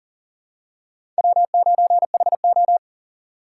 Morse code, heard as a keyed tone, about 40 wpm, 700 Hz. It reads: W9HO
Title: USA Call Signs - 4 Characters - 6x Character Spacing - 40wpm